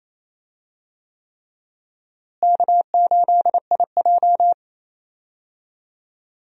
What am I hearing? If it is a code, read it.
Morse code, heard as a keyed tone, about 28 wpm, 700 Hz. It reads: K8IJ
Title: USA Call Signs - 4 Characters - Repeated 3x - 28wpm